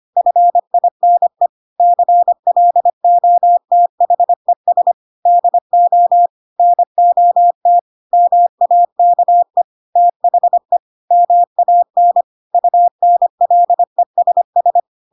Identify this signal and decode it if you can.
Morse code, heard as a keyed tone, about 25 wpm, 700 Hz. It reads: FINE CLOTHES DO NOT MAKE THE MAN UNLESS